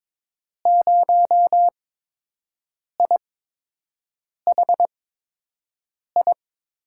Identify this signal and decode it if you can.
Morse code, heard as a keyed tone, about 21 wpm, spaced as for 6 wpm, 700 Hz. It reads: 0IHI
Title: USA Call Signs - 6 Characters - 8x Character Spacing - 22wpm